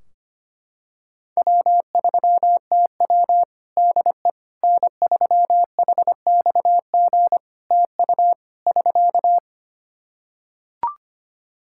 Morse code, heard as a keyed tone, about 25 wpm, 700 Hz. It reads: W3TW DE N3HXG TU <SK>
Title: Practice QSO - Exchange 4 - Ending - 25wpm